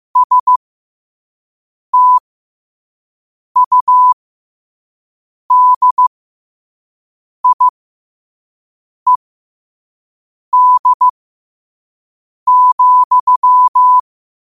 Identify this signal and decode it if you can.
Morse code, heard as a keyed tone, about 15 wpm, spaced as for 5 wpm, 1000 Hz. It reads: STUDIED,
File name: qotdfiles / 5wpm